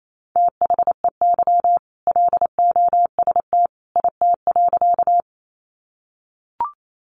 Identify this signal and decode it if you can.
Morse code, heard as a keyed tone, about 28 wpm, 700 Hz. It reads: THEY LOST IT.
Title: Sentences from Top 1400 Words 28wpm